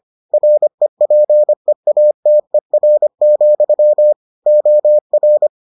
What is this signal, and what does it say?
Morse code, heard as a keyed tone, about 25 wpm, 600 Hz. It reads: REPEATER, OR